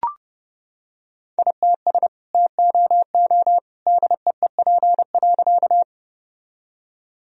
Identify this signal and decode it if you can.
Morse code, heard as a keyed tone, about 30 wpm, 700 Hz. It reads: ITS TOO DEEP.
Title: Sentences from Top 1000 Words - Review Part 3 - 30wpm